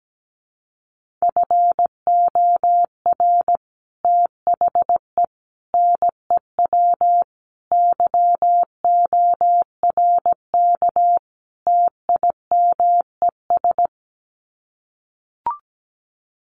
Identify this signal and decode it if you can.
Morse code, heard as a keyed tone, about 17 wpm, 700 Hz. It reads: FOR THE NEW YORK TIMES
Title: Sets of 5 Words 17wpm